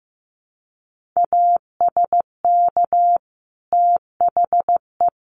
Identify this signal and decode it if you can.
Morse code, heard as a keyed tone, about 15 wpm, 700 Hz. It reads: ASK THE
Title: Sentences from Top 1500 Words 15wpm